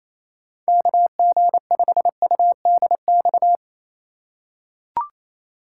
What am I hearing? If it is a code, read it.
Morse code, heard as a keyed tone, about 28 wpm, 700 Hz. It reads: KG5UDX
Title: USA Call Signs - 6 Characters - 4x Character Spacing - 28wpm